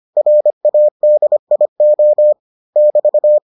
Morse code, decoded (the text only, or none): RADIO <BT>